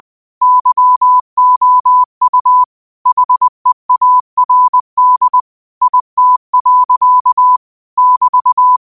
Morse code, heard as a keyed tone, about 20 wpm, 1000 Hz. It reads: YOU HEARD IT. <BT>